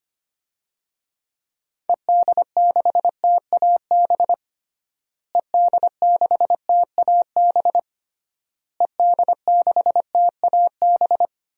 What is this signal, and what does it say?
Morse code, heard as a keyed tone, about 25 wpm, 700 Hz. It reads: ED6TAB ED6TAB ED6TAB
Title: DXCC-100 Call Signs - 6 Characters - Repeated 3x - 25wpm